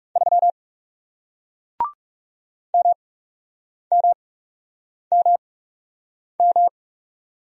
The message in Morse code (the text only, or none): 3 MMMM